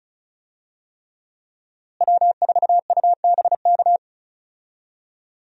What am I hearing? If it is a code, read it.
Morse code, heard as a keyed tone, about 36 wpm, 700 Hz. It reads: W4UBK
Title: USA Call Signs - 5 Characters - Repeated 3x - 35wpm